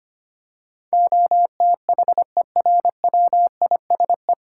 Morse code, heard as a keyed tone, about 25 wpm, 700 Hz. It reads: OTHERWISE
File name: Nine Letter Words - 25wpm